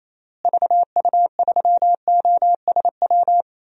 Morse code, heard as a keyed tone, about 28 wpm, 700 Hz. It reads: VU3OSW
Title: DXCC-100 Call Signs - 6 Characters - Repeated 2x - 28wpm